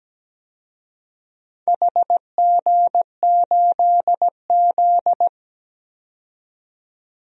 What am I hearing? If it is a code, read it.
Morse code, heard as a keyed tone, about 17 wpm, 700 Hz. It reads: HG8Z
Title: DXCC-100 Call Signs - 4 Characters - 17wpm